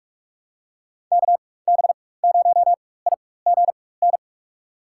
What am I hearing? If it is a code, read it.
Morse code, heard as a keyed tone, about 46 wpm, spaced as for 21 wpm, 700 Hz. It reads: KB0IGN
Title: USA Call Signs - 4x Character Spacing - 45wpm